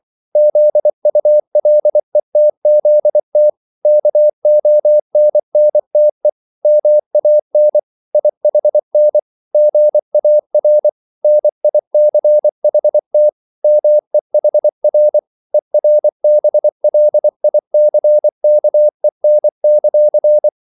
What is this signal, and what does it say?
Morse code, heard as a keyed tone, about 24 wpm, 600 Hz. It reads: ZULETZT KONNTE MAN IHN GAR NICHT MEHR ERBLICKEN;